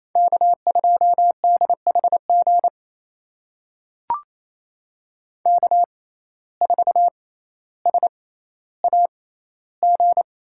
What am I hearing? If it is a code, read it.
Morse code, heard as a keyed tone, about 28 wpm, spaced as for 14 wpm, 700 Hz. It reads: K2DHG K4SAG
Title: USA Call Signs - 5 Characters - 6x Character Spacing - 28wpm